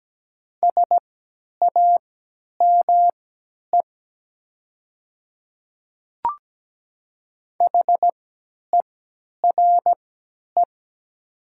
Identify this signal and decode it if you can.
Morse code, heard as a keyed tone, about 17 wpm, spaced as for 10 wpm, 700 Hz. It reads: SAME HERE
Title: 3x Character Spacing - Top 200 Words - 17wpm